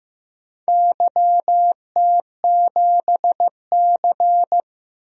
Morse code, heard as a keyed tone, about 15 wpm, 700 Hz. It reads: YT7C
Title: DXCC-100 Call Signs - 4 Characters - Repeated 3x - 15wpm